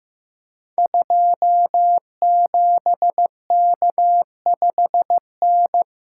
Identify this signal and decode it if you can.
Morse code, heard as a keyed tone, about 15 wpm, 700 Hz. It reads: 27K5N
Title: Code Groups - 5 Letter:Numbers - 15wpm